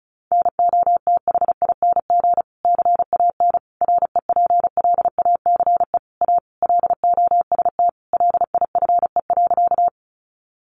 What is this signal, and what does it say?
Morse code, heard as a keyed tone, about 34 wpm, 700 Hz. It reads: NOTHING CAN REPLACE A LOST LIFE.
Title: Sentences from Top 1400 Words 35wpm